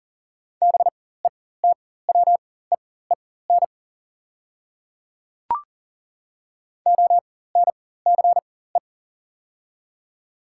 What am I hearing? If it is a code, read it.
Morse code, heard as a keyed tone, about 40 wpm, spaced as for 19 wpm, 700 Hz. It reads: BETWEEN ONCE
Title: 4x Character Spacing - Top 300 Words - 40wpm